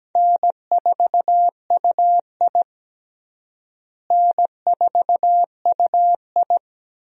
Morse code, heard as a keyed tone, about 17 wpm, 700 Hz. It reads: N4UI N4UI
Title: USA Call Signs - 4 Characters - Repeated 2x - 17wpm